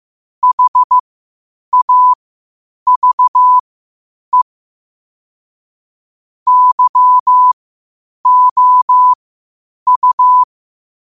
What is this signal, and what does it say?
Morse code, heard as a keyed tone, about 15 wpm, spaced as for 8 wpm, 1000 Hz. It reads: HAVE YOU
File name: qotdfiles / 8wpm